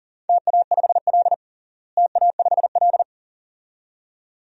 Morse code, heard as a keyed tone, about 40 wpm, 700 Hz. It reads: TA5L TA5L
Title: DXCC-100 Call Signs - 4 Characters - Repeated 2x - 40wpm